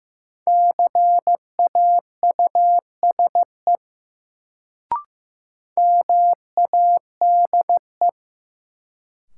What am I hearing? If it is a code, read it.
Morse code, heard as a keyed tone, about 15 wpm, 700 Hz. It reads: CAUSE MADE
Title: Sentences from Top 200 Words - New Word Drill - 15wpm